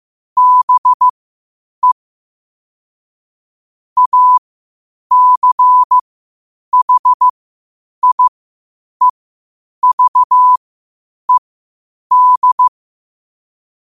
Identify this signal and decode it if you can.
Morse code, heard as a keyed tone, about 15 wpm, spaced as for 8 wpm, 1000 Hz. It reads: BE ACHIEVED
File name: qotdfiles / 8wpm